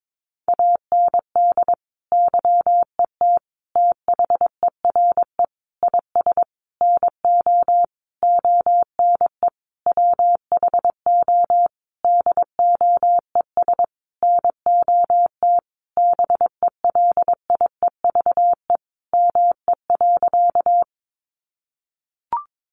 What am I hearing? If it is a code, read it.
Morse code, heard as a keyed tone, about 22 wpm, 700 Hz. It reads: AND YET THERE IS NO ONE WHO DOES NOT BELIEVE ME.
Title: Sentences from Top 1000 Words 22wpm